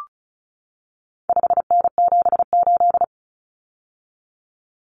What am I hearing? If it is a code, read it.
Morse code, heard as a keyed tone, about 34 wpm, 700 Hz. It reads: 5N78
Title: Common Cut Numbers - 4-Digit 35wpm